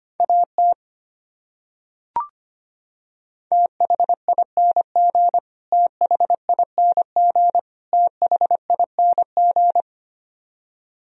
Morse code, heard as a keyed tone, about 25 wpm, 700 Hz. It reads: AT THING THING THING
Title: Top 100 Words Repeated 25wpm